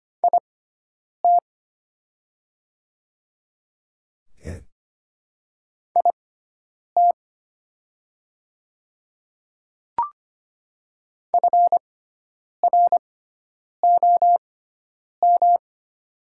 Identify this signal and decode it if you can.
Morse code, heard as a keyed tone, about 25 wpm, spaced as for 9 wpm, 700 Hz. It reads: IT IT FROM
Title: 6x Character Spacing - Top 100 Words - 25wpm